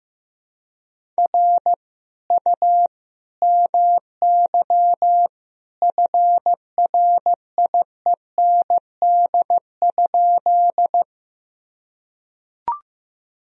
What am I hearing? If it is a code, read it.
Morse code, heard as a keyed tone, about 15 wpm, 700 Hz. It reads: R U MY FRIEND?
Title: Sentences from Top 300 Words and Abbreviations 15wpm